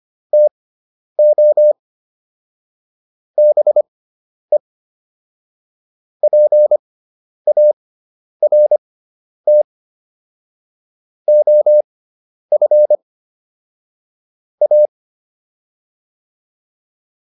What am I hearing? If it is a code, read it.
Morse code, heard as a keyed tone, about 25 wpm, spaced as for 10 wpm, 600 Hz. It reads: TO BE PART OF A